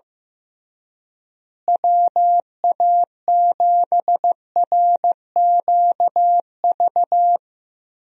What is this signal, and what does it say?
Morse code, heard as a keyed tone, about 16 wpm, 700 Hz. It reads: WA7RQV
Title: USA Call Signs - 6 Characters - Repeated 3x - 15wpm